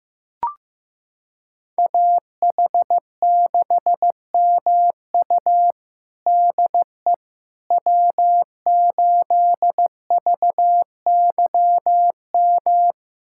Morse code, heard as a keyed tone, about 15 wpm, 700 Hz. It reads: AH6MU DE W8VYM
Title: USA Callsign DE Callsign - 5 Characters - 15wpm